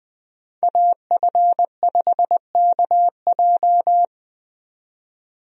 Morse code, heard as a keyed tone, about 20 wpm, 700 Hz. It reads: AF5KJ
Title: USA Call Signs - 5 Characters - 20wpm